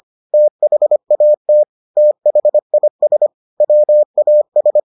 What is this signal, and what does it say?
Morse code, heard as a keyed tone, about 25 wpm, 600 Hz. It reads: THAT THIS WAS